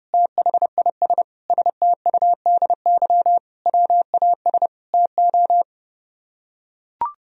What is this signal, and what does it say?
Morse code, heard as a keyed tone, about 30 wpm, 700 Hz. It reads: THIS STUDY WAS TO